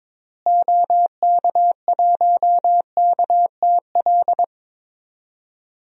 Morse code, heard as a keyed tone, about 22 wpm, 700 Hz. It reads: OK1KTL
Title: DXCC-100 Call Signs - 6 Characters - Repeated 2x - 22wpm